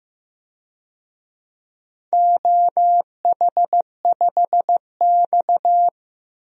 Morse code, heard as a keyed tone, about 15 wpm, 700 Hz. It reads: OH5X
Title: DXCC-100 Call Signs - 4 Characters - 15wpm